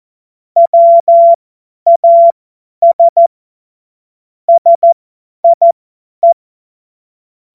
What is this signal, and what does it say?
Morse code, heard as a keyed tone, about 14 wpm, spaced as for 10 wpm, 680 Hz. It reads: WAS SIE
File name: Morsekurs / Witze / Witz 19 März 2024 50 F70